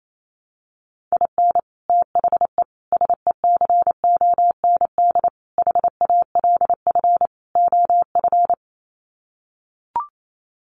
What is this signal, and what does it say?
Morse code, heard as a keyed tone, about 28 wpm, 700 Hz. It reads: IN THE SECOND HALF OF
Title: Sets of 5 Words 28wpm